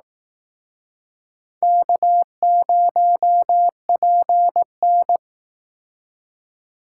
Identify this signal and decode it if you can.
Morse code, heard as a keyed tone, about 18 wpm, 700 Hz. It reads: K0PN
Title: Speed Racing - USA 1x2 Call Signs 15wpm